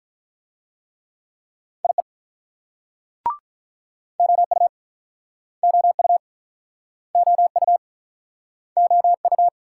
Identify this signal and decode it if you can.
Morse code, heard as a keyed tone, about 44 wpm, 700 Hz. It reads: IE OU OU OU OU